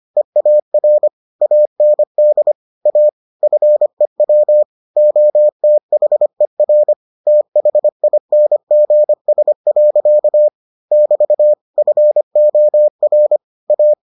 Morse code, heard as a keyed tone, about 25 wpm, 600 Hz. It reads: EAR AND A FEW OTHER THINGS. <BT> FOR A